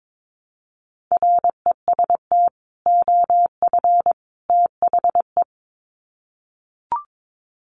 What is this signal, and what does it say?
Morse code, heard as a keyed tone, about 22 wpm, 700 Hz. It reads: REST OF THE
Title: Sets of 3 Words 22wpm